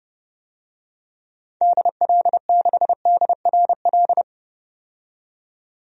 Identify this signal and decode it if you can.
Morse code, heard as a keyed tone, about 31 wpm, 700 Hz. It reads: DL6DRL
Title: DXCC-100 Call Signs - 6 Characters - 30wpm